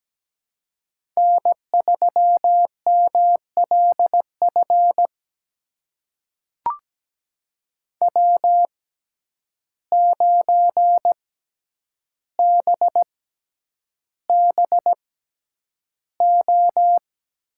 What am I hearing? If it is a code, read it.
Morse code, heard as a keyed tone, about 17 wpm, spaced as for 9 wpm, 700 Hz. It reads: N3MLF W9BBO
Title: USA Call Signs - 5 Characters - 6x Character Spacing - 17wpm